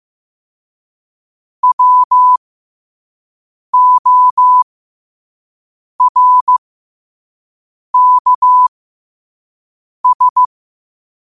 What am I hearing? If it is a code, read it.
Morse code, heard as a keyed tone, about 15 wpm, spaced as for 5 wpm, 1000 Hz. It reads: WORKS
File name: qotdfiles / 5wpm